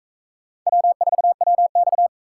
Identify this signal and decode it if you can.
Morse code, heard as a keyed tone, about 38 wpm, 700 Hz. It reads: W4WX